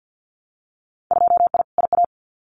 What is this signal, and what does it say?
Morse code, heard as a keyed tone, about 50 wpm, 700 Hz. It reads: 2I IA
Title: Winter Field Day Exchanges Encore 50wpm